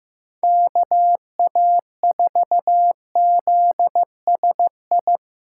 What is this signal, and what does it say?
Morse code, heard as a keyed tone, about 15 wpm, 700 Hz. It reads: KA4ZSI
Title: USA Call Signs - 6 Characters - 3x Character Spacing - 15wpm